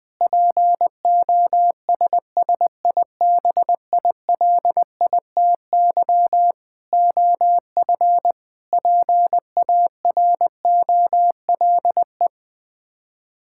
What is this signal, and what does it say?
Morse code, heard as a keyed tone, about 20 wpm, 700 Hz. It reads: POSSIBILITY OF PAROLE
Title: Sets of 8 Words - 20wpm